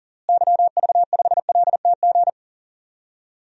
Morse code, heard as a keyed tone, about 40 wpm, 700 Hz. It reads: YV5LTG
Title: DXCC-100 Call Signs - 6 Characters - Repeated 3x - 40wpm